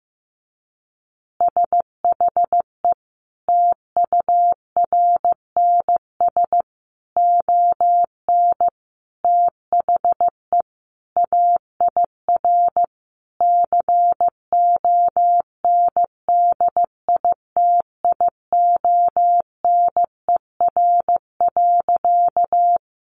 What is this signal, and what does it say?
Morse code, heard as a keyed tone, about 14 wpm, 700 Hz. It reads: SHE TURNS ON THE AIR CONDITIONER.
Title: Easy Phrases 15wpm